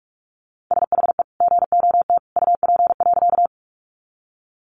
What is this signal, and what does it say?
Morse code, heard as a keyed tone, about 44 wpm, 700 Hz. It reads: SHE GOT UP.
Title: Sentences from Top 300 Words 45wpm